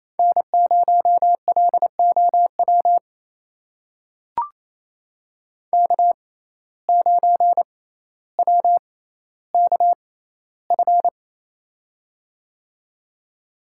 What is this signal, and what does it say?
Morse code, heard as a keyed tone, about 28 wpm, spaced as for 14 wpm, 700 Hz. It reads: N0LOW K9WKF
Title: USA Call Signs - 5 Characters - 6x Character Spacing - 28wpm